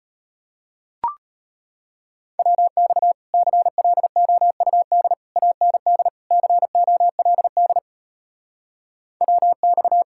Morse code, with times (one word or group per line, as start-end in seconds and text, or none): 2.4-3.1 WX
3.3-5.1 CLOUD
5.4-6.1 AND
6.3-7.8 COLD
9.2-10.0 WX